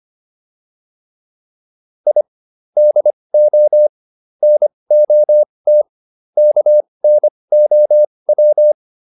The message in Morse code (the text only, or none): I DO NOT KNOW